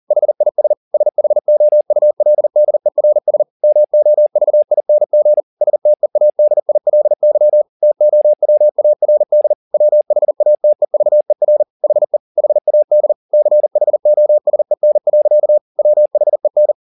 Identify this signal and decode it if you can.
Morse code, heard as a keyed tone, about 40 wpm, 600 Hz. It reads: HIS SHOULDERS MOVING STEADILY TOWARD WHATEVER HE HAD CHOSEN. WHEN